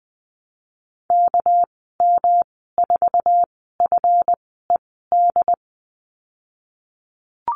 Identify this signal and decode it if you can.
Morse code, heard as a keyed tone, about 20 wpm, spaced as for 14 wpm, 700 Hz. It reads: KM4FED
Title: 231 Lesson 37g - Letter x - Call Signs 20wpm